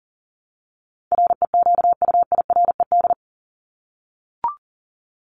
Morse code, heard as a keyed tone, about 40 wpm, 700 Hz. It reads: REQUIRED
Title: ED Suffix - Top 100 Words with Base Words 40wpm